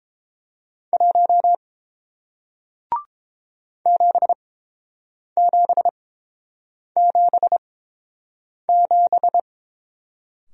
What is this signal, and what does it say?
Morse code, heard as a keyed tone, about 30 wpm, spaced as for 7 wpm, 700 Hz. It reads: 1 7777